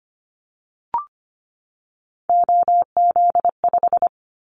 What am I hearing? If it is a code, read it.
Morse code, heard as a keyed tone, about 25 wpm, 690 Hz. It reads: OZ5
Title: Callsign Prefixes for Top 50 DXCC - 25wpm